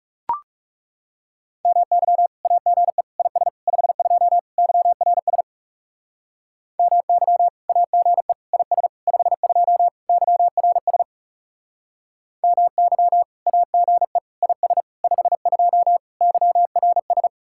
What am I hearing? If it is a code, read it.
Morse code, heard as a keyed tone, about 38 wpm, 700 Hz. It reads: MY AGE IS 52 YRS MY AGE IS 52 YRS MY AGE IS 52 YRS